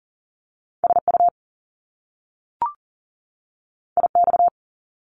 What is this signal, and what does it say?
Morse code, heard as a keyed tone, about 38 wpm, 700 Hz. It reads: SU IX
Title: Top Two Letter Combinations in Words 40wpm